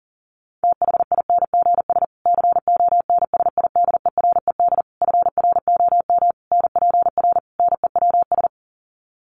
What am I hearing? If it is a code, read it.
Morse code, heard as a keyed tone, about 40 wpm, 700 Hz. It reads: THINGS CONSIDERED FROM NPR NEWS
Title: Sets of 5 Words Encore 40wpm